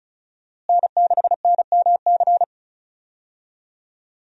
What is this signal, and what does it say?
Morse code, heard as a keyed tone, about 36 wpm, 700 Hz. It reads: N6NMC
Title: USA Call Signs - 5 Characters - Repeated 2x - 35wpm